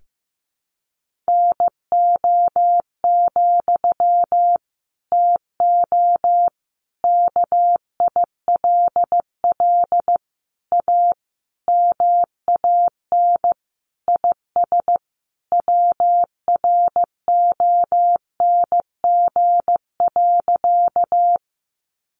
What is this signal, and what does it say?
Morse code, heard as a keyed tone, about 15 wpm, 700 Hz. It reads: NO, TO KILL A MAN IS WRONG.
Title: Sentences from Top 1400 Words 15wpm